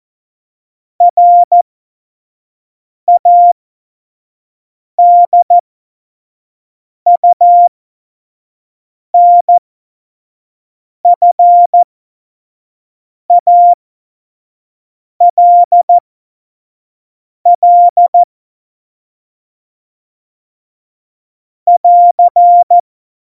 Morse code, heard as a keyed tone, about 14 wpm, spaced as for 5 wpm, 700 Hz. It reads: RADUNFALL <AR>
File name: Morsekurs / Witze / Witz 22. September 2020, 25 F70